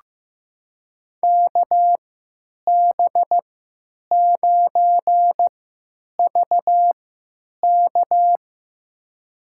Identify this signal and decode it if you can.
Morse code, heard as a keyed tone, about 15 wpm, spaced as for 9 wpm, 700 Hz. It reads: KB9VK
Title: USA Call Signs - 5 Characters - 3x Character Spacing - 15wpm